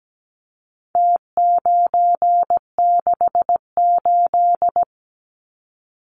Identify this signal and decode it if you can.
Morse code, heard as a keyed tone, about 17 wpm, 700 Hz. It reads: T968
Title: Numbers - Contest Style - 4-Digit 17wpm